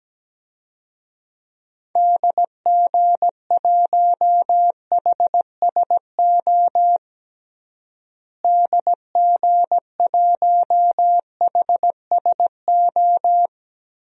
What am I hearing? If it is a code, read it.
Morse code, heard as a keyed tone, about 17 wpm, 700 Hz. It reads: DG1HSO DG1HSO
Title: DXCC-100 Call Signs - 6 Characters - Repeated 2x - 17wpm